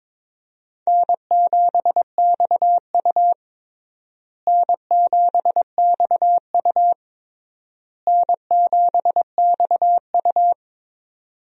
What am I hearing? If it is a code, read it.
Morse code, heard as a keyed tone, about 22 wpm, 700 Hz. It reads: N7XU N7XU N7XU
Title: USA Call Signs - 4 Characters - Repeated 3x - 22wpm